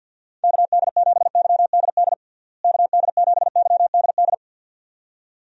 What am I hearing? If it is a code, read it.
Morse code, heard as a keyed tone, about 51 wpm, 700 Hz. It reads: KD7YDD KD7YDD
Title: USA Call Signs - 6 Characters - Repeated 2x - 50wpm